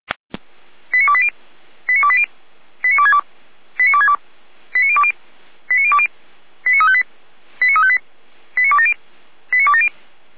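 Five-tone selective call (ZVEI), heard as quick runs of five tones, tones of about 70 ms. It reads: 89280 89280 89262 89262 89020 89020 89358 89358 89270 89270
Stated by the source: Five-tone ZVEI selective call received by SDR, audio from feuerwehr-orlamuende.org (Feuerwehr Orlamünde).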